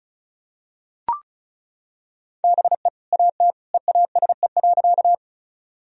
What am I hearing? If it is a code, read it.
Morse code, heard as a keyed tone, about 34 wpm, 700 Hz. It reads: BE AT EASE.
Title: Sentences from Top 1000 Words - Review Part 1 - 35wpm